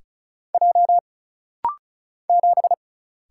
Morse code, heard as a keyed tone, about 36 wpm, 700 Hz. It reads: J 7
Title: Single Letter-Number - Rapid-Fire 35wpm